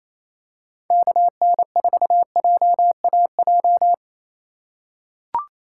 Morse code, heard as a keyed tone, about 28 wpm, 700 Hz. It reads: KN4JAJ
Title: USA Call Signs - 6 Characters - 2x Character Spacing - 28wpm